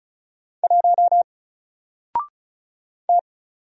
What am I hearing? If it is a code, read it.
Morse code, heard as a keyed tone, about 36 wpm, 700 Hz. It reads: 1 T
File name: Single-Letter-Number-35wpm